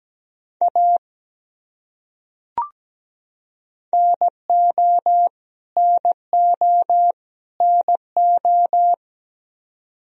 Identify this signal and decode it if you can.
Morse code, heard as a keyed tone, about 17 wpm, 700 Hz. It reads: A NO NO NO
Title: Top 100 Words Repeated 17wpm